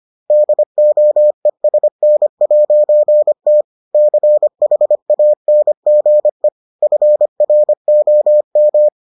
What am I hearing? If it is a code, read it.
Morse code, heard as a keyed tone, about 25 wpm, 600 Hz. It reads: DOESN'T CHANGE FROM